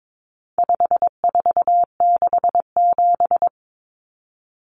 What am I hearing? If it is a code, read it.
Morse code, heard as a keyed tone, about 22 wpm, 700 Hz. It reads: 5467
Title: Common Cut Numbers - 4-Digit 22wpm